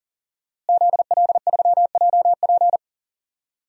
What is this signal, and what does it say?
Morse code, heard as a keyed tone, about 40 wpm, 700 Hz. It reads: ZL3JP
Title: DXCC-100 Call Signs - 5 Characters - Repeated 2x - 40wpm